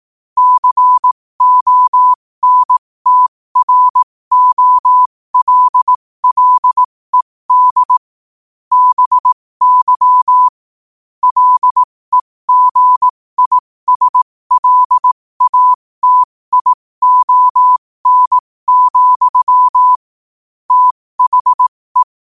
Morse code, heard as a keyed tone, about 18 wpm, spaced as for 15 wpm, 1000 Hz. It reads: CONTROLLED BY LEGISLATION, THE
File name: qotdfiles / 15wpm